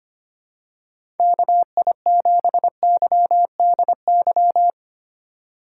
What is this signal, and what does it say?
Morse code, heard as a keyed tone, about 25 wpm, 700 Hz. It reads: KI7YDY
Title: USA Call Signs - 6 Characters - 8x Character Spacing - 25wpm